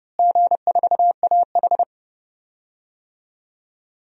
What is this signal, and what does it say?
Morse code, heard as a keyed tone, about 31 wpm, 700 Hz. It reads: G4AH
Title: DXCC-100 Call Signs - 4 Characters - Repeated 3x - 30wpm